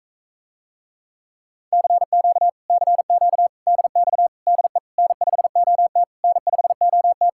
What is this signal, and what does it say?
Morse code, heard as a keyed tone, about 45 wpm, 700 Hz. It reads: CQ CQ DX DE N5OT N5OT